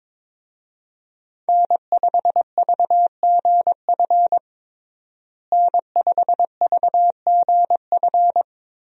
Morse code, heard as a keyed tone, about 22 wpm, 700 Hz. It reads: N5VGF N5VGF
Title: USA Call Signs - 5 Characters - Repeated 3x - 22wpm